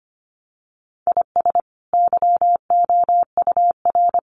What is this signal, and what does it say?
Morse code, heard as a keyed tone, about 25 wpm, 700 Hz. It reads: IS YOUR
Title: Sentences from Top 400 Words 25wpm